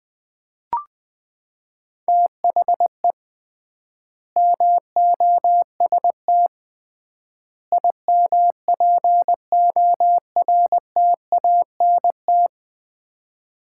Morse code, heard as a keyed tone, about 20 wpm, 700 Hz. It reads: THE MOST IMPORTANT
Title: 3x Word-Spacing - Sets of 3 Words - 20wpm